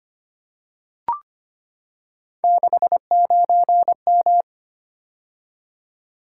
Morse code, heard as a keyed tone, about 25 wpm, 700 Hz. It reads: 69M
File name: Three Letter-Number Combinations - 25wpm